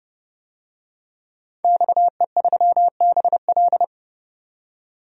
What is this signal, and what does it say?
Morse code, heard as a keyed tone, about 31 wpm, 700 Hz. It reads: XE3BL
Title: DXCC-100 Call Signs - 5 Characters - 30wpm